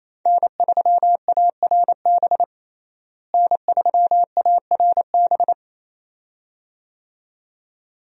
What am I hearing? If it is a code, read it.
Morse code, heard as a keyed tone, about 28 wpm, 700 Hz. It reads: N3ARB N3ARB
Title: USA Call Signs - 5 Characters - Repeated 2x - 28wpm